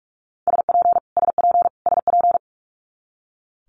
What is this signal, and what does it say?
Morse code, heard as a keyed tone, about 46 wpm, 700 Hz. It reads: SP SP SP
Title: Multi-letter Phonemes Repeated 3x 45wpm